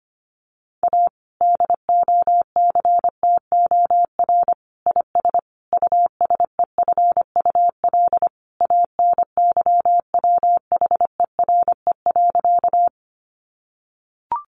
Morse code, heard as a keyed tone, about 25 wpm, 700 Hz. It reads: A DOCTOR IS USEFUL ANYWHERE.
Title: Sentences from Top 1500 Words 25wpm